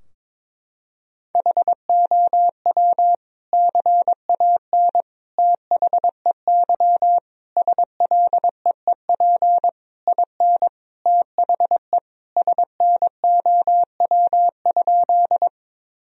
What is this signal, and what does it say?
Morse code, heard as a keyed tone, about 22 wpm, 700 Hz. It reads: HOW CAN THEY SLEEP IN THE SNOW?
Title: Sentences from Top 1000 Words - Review Part 3 - 22wpm